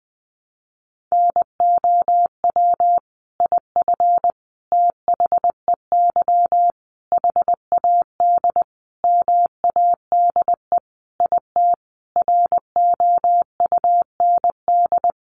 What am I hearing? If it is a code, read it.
Morse code, heard as a keyed tone, about 20 wpm, 700 Hz. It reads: NOW IF THEY HAD MADE IT ROUND
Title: Sentences from Top 500 Words 20wpm